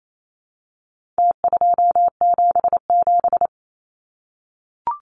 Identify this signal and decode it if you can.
Morse code, heard as a keyed tone, about 28 wpm, 700 Hz. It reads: T277
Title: Numbers - Contest Style - 4-Digit 28wpm